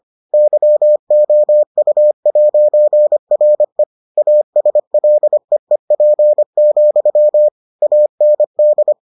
Morse code, heard as a keyed tone, about 25 wpm, 600 Hz. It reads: YOU'RE ASLEEP, AND